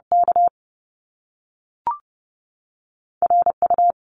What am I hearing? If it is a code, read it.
Morse code, heard as a keyed tone, about 30 wpm, 700 Hz. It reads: K R U
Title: Top Two Letter Combinations in Words 30wpm